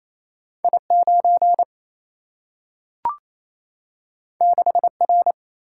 Morse code, heard as a keyed tone, about 28 wpm, 700 Hz. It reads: I9 6R
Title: Single-Letter Single-Number 28wpm